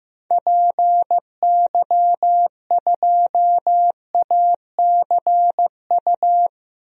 Morse code, heard as a keyed tone, about 15 wpm, 700 Hz. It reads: PY2ACU
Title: DXCC-100 Call Signs - 6 Characters - Repeated 3x - 15wpm